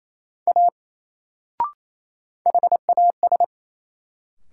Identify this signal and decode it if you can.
Morse code, heard as a keyed tone, about 28 wpm, 700 Hz. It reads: A HAS
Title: Sentences from Top 100 Words - New Word Drill - 28wpm